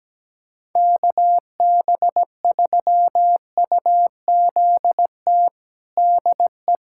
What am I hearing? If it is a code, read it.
Morse code, heard as a keyed tone, about 17 wpm, 700 Hz. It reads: KB3UZT DE
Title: USA Callsign DE Callsign 17wpm